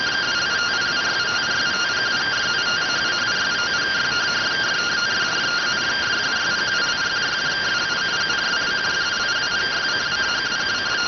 Two-tone teleprinter signal, about 170 Hz shift.